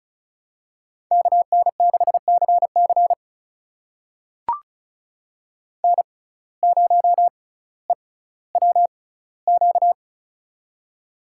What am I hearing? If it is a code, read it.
Morse code, heard as a keyed tone, about 34 wpm, spaced as for 18 wpm, 700 Hz. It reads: KN6CC N0EWQ
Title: USA Call Signs - 5 Characters - 6x Character Spacing - 35wpm